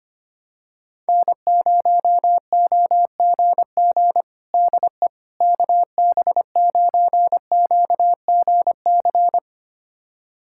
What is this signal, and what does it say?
Morse code, heard as a keyed tone, about 25 wpm, 700 Hz. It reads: N0OGG DE KB9QGC
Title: USA Callsign DE Callsign 25wpm